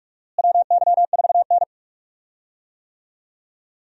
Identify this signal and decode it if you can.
Morse code, heard as a keyed tone, about 46 wpm, 700 Hz. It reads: WY4N
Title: USA Call Signs - 4 Characters - Repeated 2x - 45wpm